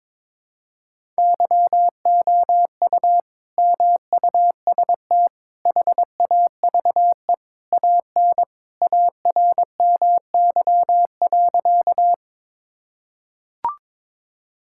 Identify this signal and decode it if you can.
Morse code, heard as a keyed tone, about 22 wpm, 700 Hz. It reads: YOU MUST HAVE AN ARMY.
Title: Sentences from Top 1000 Words - Review Part 5 - 22wpm